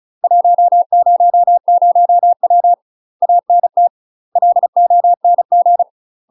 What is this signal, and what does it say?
Morse code, heard as a keyed tone, about 34 wpm, 700 Hz. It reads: 100W ANT LONG
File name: UK MASTER QSOs TRAINING / 35WPM / UK MASTER QSOs 0037